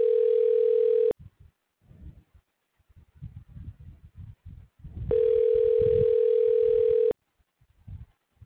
Ringback tone (North America), 440 + 480 Hz.